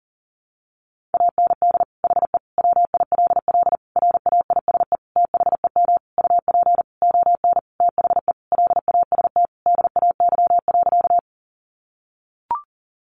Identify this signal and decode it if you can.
Morse code, heard as a keyed tone, about 40 wpm, 700 Hz. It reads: AND HE WILL RAISE THEM UP ON THE LAST DAY.
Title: Sentences from Top 1400 Words 40wpm